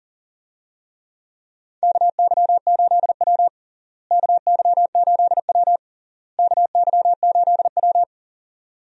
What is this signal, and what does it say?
Morse code, heard as a keyed tone, about 37 wpm, 700 Hz. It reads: KY8W KY8W KY8W
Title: USA Call Signs - 4 Characters - Repeated 3x - 40wpm